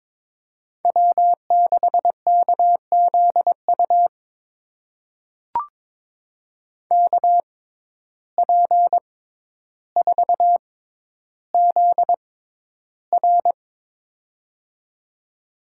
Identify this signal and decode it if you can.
Morse code, heard as a keyed tone, about 22 wpm, spaced as for 11 wpm, 700 Hz. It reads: W6KZU KP4ZR
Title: USA Call Signs - 5 Characters - 6x Character Spacing - 22wpm